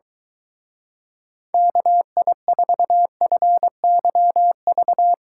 Morse code, heard as a keyed tone, about 23 wpm, 700 Hz. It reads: KI4FYV